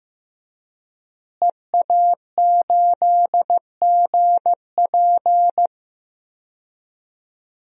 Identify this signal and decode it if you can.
Morse code, heard as a keyed tone, about 15 wpm, 700 Hz. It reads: EA8GP
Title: DXCC-100 Call Signs - 5 Characters - Repeated 2x - 15wpm